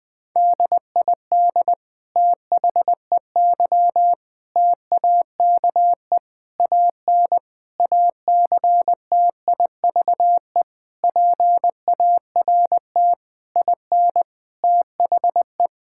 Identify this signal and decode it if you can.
Morse code, heard as a keyed tone, about 21 wpm, 700 Hz. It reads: DID THEY TAKE AN ACTIVE PART IN THE
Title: Sentences from Top 1000 Words - Review Part 6 - 20wpm